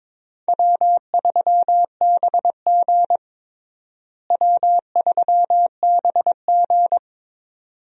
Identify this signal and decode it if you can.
Morse code, heard as a keyed tone, about 22 wpm, 700 Hz. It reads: W3BG W3BG
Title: USA Call Signs - 4 Characters - Repeated 2x - 22wpm